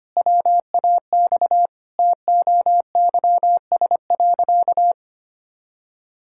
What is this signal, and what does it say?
Morse code, heard as a keyed tone, about 25 wpm, 700 Hz. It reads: WAX TOYS.
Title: Pangram Challenge v2 - 25wpm